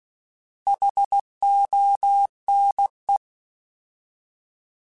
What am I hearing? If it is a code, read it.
Morse code, heard as a keyed tone, about 16 wpm, 800 Hz. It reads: HONE
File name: Alphabetical / Alphabetical mp3s / Hs @15